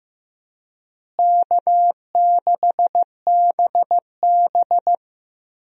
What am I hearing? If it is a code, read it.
Morse code, heard as a keyed tone, about 15 wpm, 700 Hz. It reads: K6BB
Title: USA Call Signs - 4 Characters - Repeated 3x - 15wpm